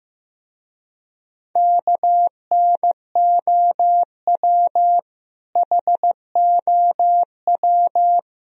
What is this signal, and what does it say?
Morse code, heard as a keyed tone, about 16 wpm, 700 Hz. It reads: KNOW HOW
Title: Speed-Racing - Sets of 3 Words 15wpm